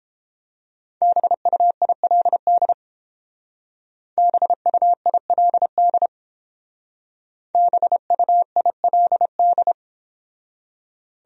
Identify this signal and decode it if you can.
Morse code, heard as a keyed tone, about 30 wpm, 700 Hz. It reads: BUILD BUILD BUILD